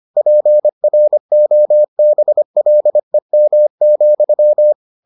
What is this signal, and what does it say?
Morse code, heard as a keyed tone, about 25 wpm, 600 Hz. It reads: PROBLEM,